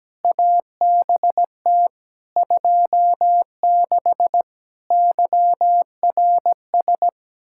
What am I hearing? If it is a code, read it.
Morse code, heard as a keyed tone, about 17 wpm, 700 Hz. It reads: ABT 26 YRS
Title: Practice QSO - Exchange 3 - More Info - 17wpm